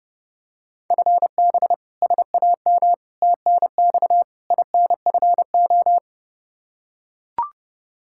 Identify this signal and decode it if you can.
Morse code, heard as a keyed tone, about 32 wpm, 700 Hz. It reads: FB SAM TNX INFO
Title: Practice QSO - Exchange 3 - More Info - 30wpm